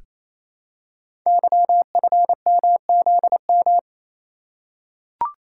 Morse code, heard as a keyed tone, about 27 wpm, 700 Hz. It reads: YFMZM
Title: Code Groups - 5 Letters - 28wpm